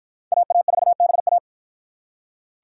Morse code, heard as a keyed tone, about 49 wpm, 700 Hz. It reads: AA4BA